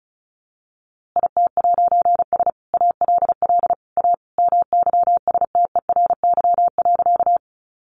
Morse code, heard as a keyed tone, about 34 wpm, 700 Hz. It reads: IT'S ALL A MYSTERY.
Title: A Sound Of Thunder 35wpm